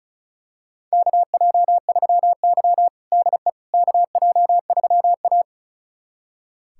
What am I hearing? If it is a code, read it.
Morse code, heard as a keyed tone, about 34 wpm, 700 Hz. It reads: KJ3Y DE KJ3A
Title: USA Callsign DE Callsign - 4 Characters - 35wpm